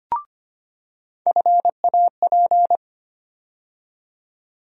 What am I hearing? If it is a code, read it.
Morse code, heard as a keyed tone, about 25 wpm, 700 Hz. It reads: FAP
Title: USA Call Sign Postfixes - 3 Letters 25wpm